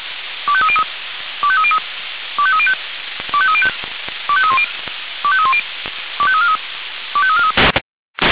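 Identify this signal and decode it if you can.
Five-tone selective call (ZVEI), heard as quick runs of five tones, tones of about 70 ms. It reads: 25303 25303 25305 25305 25310 25310 25343 25343
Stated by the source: ZVEI five-tone call 25299 received on SDR.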